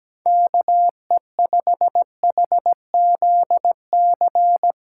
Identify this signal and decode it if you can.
Morse code, heard as a keyed tone, about 17 wpm, 700 Hz. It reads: KE5HZC
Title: USA Callsign DE Callsign - 6 Characters - 17wpm